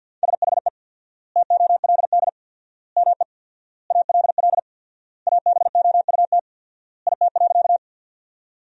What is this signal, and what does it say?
Morse code, heard as a keyed tone, about 52 wpm, 700 Hz. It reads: SHE TOLD ME ALL ABOUT IT.
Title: 4x Word-Spacing - Sentences from Top 400 Words - 50wpm